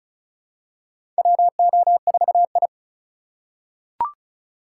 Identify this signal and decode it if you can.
Morse code, heard as a keyed tone, about 34 wpm, 700 Hz. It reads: WO4I
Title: USA Call Signs - 4 Characters - 4x Character Spacing - 35wpm